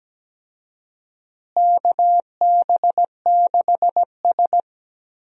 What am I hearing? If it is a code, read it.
Morse code, heard as a keyed tone, about 17 wpm, 700 Hz. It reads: KB6S
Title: USA Call Signs - 4 Characters - Repeated 3x - 17wpm